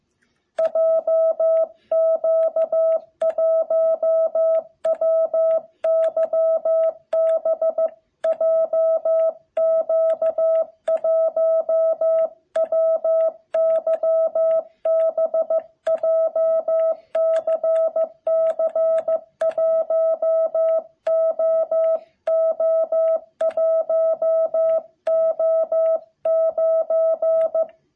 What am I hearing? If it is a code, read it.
Morse code, heard as a keyed tone, about 15 wpm, 660 Hz. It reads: JQ1WYB JQ1WYB JCC1OO1O9
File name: jh8cbh / 2021cw